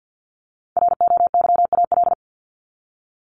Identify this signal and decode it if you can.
Morse code, heard as a keyed tone, about 50 wpm, 700 Hz. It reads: ROYAL
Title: Top 1751-2000 Words 50wpm